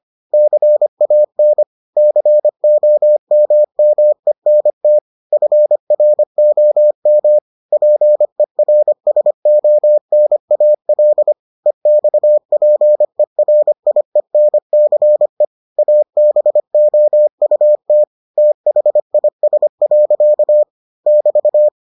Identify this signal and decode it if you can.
Morse code, heard as a keyed tone, about 25 wpm, 600 Hz. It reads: CAN COMMENT FROM PERSONAL EXPERIENCE ABOUT THIS. <BT>